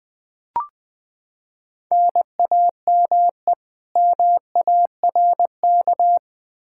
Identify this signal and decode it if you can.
Morse code, heard as a keyed tone, about 20 wpm, 700 Hz. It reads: NAME MARK
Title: Practice QSO - Exchange 1 - Beginning - 20wpm